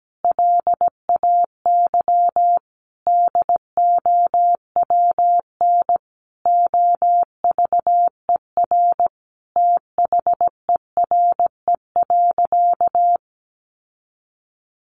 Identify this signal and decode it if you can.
Morse code, heard as a keyed tone, about 17 wpm, 700 Hz. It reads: LAY DOWN OVER THERE.
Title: Sentences from Top 800 Words 17wpm